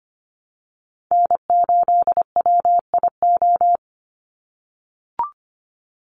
Morse code, heard as a keyed tone, about 25 wpm, 700 Hz. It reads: N8WIO
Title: USA Call Signs 25wpm